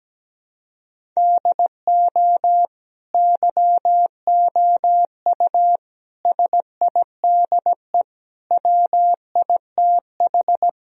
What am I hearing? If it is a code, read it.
Morse code, heard as a keyed tone, about 17 wpm, 700 Hz. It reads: DO YOU SIDE WITH
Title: Sentences from Top 1000 Words - Review Part 1 - 17wpm